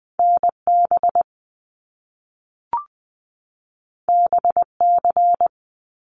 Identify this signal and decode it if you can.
Morse code, heard as a keyed tone, about 19 wpm, 700 Hz. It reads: NB BC
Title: Canadian Provinces and Territories 20wpm